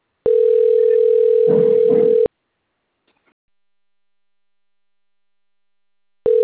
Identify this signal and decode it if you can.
Ringback tone (North America), 440 + 480 Hz.
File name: rg-1600-unknown- - - .39150